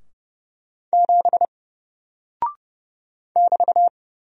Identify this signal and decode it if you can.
Morse code, heard as a keyed tone, about 30 wpm, 700 Hz. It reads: 7 <BT>
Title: Single Character (Letter:Number:Punctuation:Prosign) Alt 30wpm